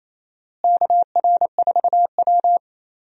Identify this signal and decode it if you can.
Morse code, heard as a keyed tone, about 28 wpm, 700 Hz. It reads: KR4W
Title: USA Call Signs - 4 Characters - Repeated 3x - 28wpm